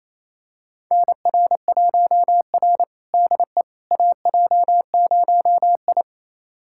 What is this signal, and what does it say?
Morse code, heard as a keyed tone, about 28 wpm, 700 Hz. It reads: NR1R DE AJ0I
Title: USA Callsign DE Callsign - 4 Characters - 28wpm